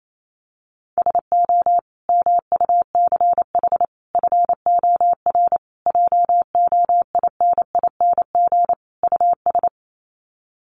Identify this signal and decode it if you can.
Morse code, heard as a keyed tone, about 28 wpm, 700 Hz. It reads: SO MUCH FOR JOINING US
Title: Sets of 5 Words 28wpm